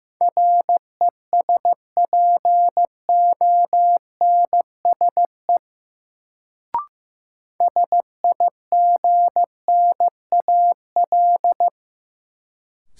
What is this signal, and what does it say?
Morse code, heard as a keyed tone, about 16 wpm, 700 Hz. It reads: RESPONSE SIGNAL
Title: Sentences from Top 1400 Words - New Word Drill Alt - 15wpm